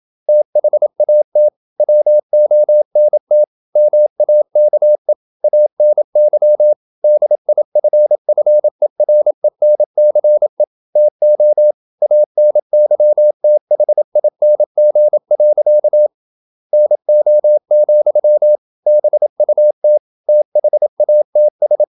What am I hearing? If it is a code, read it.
Morse code, heard as a keyed tone, about 28 wpm, 600 Hz. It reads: THAT WONT MAKE ANY DIFFERENCE TO ANYTHING. NO, BUT THATS
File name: Never 0027wpm